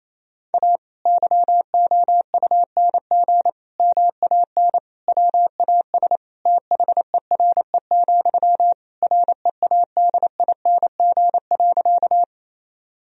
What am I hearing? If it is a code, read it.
Morse code, heard as a keyed tone, about 27 wpm, 700 Hz. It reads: A YOUNG MAN WAS THERE, READING.
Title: Sentences from Top 1000 Words - Review Part 3 - 28wpm